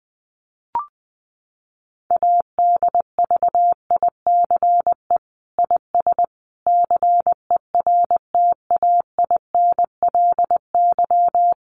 Morse code, heard as a keyed tone, about 20 wpm, 700 Hz. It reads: ADVICE IS CERTAINLY
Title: Sentences from Top 1100 Words 20wpm